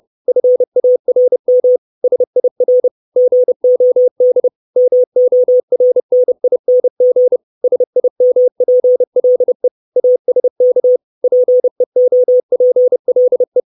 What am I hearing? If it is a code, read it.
Morse code, heard as a keyed tone, about 30 wpm, 500 Hz. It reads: FARM SIR GOD MORNING SIMPLE ASK PEOPLE
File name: 632 Words 30WPM